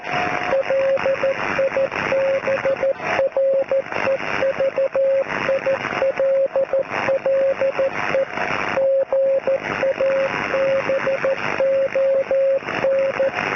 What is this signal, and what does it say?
Morse code, heard as a keyed tone, about 14 wpm, 540 Hz, with radio noise. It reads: LIBREVILLE GABON